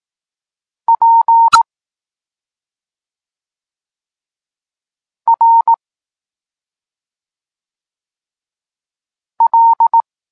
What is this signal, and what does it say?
Morse code, heard as a keyed tone, about 18 wpm, 920 Hz. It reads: P R L
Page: Tekst 32 – Doskonalenie odbioru poznanych liter